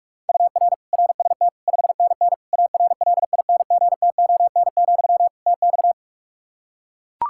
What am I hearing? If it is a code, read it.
Morse code, heard as a keyed tone, about 41 wpm, 700 Hz. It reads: UR RST 5NN ARLINGTON, TX
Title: Practice QSO - Exchange 1 - Beginning - 45wpm